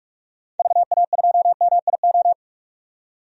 Morse code, heard as a keyed tone, about 42 wpm, 700 Hz. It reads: VA2MIO
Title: DXCC-100 Call Signs - 6 Characters - 45wpm